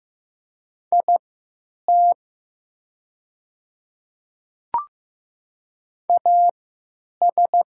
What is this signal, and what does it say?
Morse code, heard as a keyed tone, about 15 wpm, spaced as for 9 wpm, 700 Hz. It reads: IT AS
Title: 3x Character Spacing - Top 500 Words - 15wpm